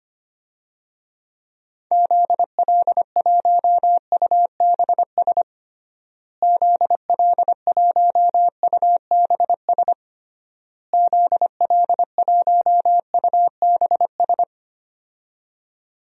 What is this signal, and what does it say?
Morse code, heard as a keyed tone, about 25 wpm, 700 Hz. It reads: ZL1UBS ZL1UBS ZL1UBS
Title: DXCC-100 Call Signs - 6 Characters - Repeated 3x - 25wpm